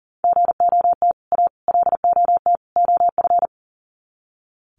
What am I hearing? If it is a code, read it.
Morse code, heard as a keyed tone, about 40 wpm, 700 Hz. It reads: GOT A LOT OF
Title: Sets of 4 Words 40wpm